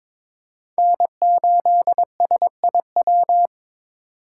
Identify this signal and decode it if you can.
Morse code, heard as a keyed tone, about 22 wpm, 700 Hz. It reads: N8SIW
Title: USA Call Signs - 5 Characters - 4x Character Spacing - 22wpm